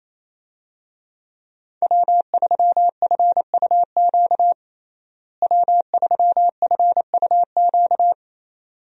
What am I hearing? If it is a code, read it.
Morse code, heard as a keyed tone, about 28 wpm, 700 Hz. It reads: W3FUQ W3FUQ
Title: USA Call Signs - 5 Characters - Repeated 2x - 28wpm